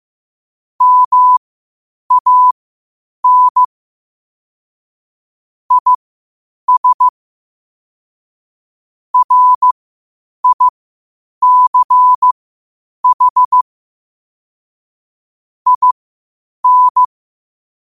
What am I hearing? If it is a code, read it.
Morse code, heard as a keyed tone, about 15 wpm, spaced as for 8 wpm, 1000 Hz. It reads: MAN IS RICH IN
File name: qotdfiles / 8wpm